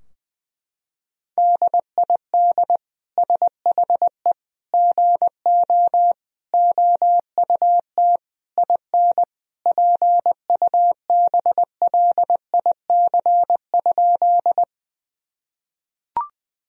Morse code, heard as a keyed tone, about 21 wpm, 700 Hz. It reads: DID SHE GO OUT IN PUBLIC?
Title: Sentences from Top 1000 Words - Review Part 3 - 20wpm